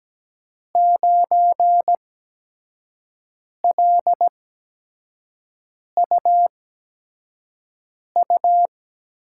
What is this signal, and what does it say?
Morse code, heard as a keyed tone, about 17 wpm, spaced as for 5 wpm, 700 Hz. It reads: 9LUU
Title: USA Call Signs - 5 Characters - 8x Character Spacing - 17wpm